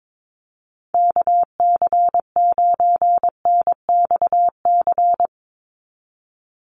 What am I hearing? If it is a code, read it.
Morse code, heard as a keyed tone, about 22 wpm, 700 Hz. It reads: KC9NXC
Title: USA Call Signs 22wpm